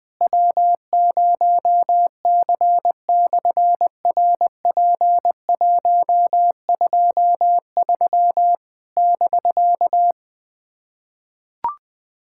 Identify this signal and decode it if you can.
Morse code, heard as a keyed tone, about 20 wpm, 700 Hz. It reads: W0C/RP123 <BK>
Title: SOTA QSO Practice 20wpm